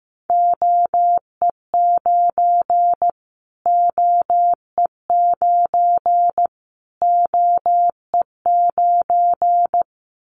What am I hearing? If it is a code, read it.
Morse code, heard as a keyed tone, about 15 wpm, 700 Hz. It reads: OE9 OE9 OE9
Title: Callsign Prefixes for Top 25 DXCC - Repeated 3x - 15wpm